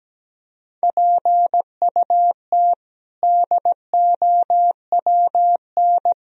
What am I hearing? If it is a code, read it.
Morse code, heard as a keyed tone, about 17 wpm, 700 Hz. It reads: PUT DOWN
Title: Sentences from Top 1000 Words - Review Part 1 - 17wpm